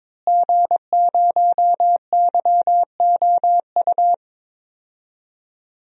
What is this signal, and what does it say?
Morse code, heard as a keyed tone, about 22 wpm, 700 Hz. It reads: G0YOU
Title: DXCC-100 Call Signs - 5 Characters - Repeated 2x - 22wpm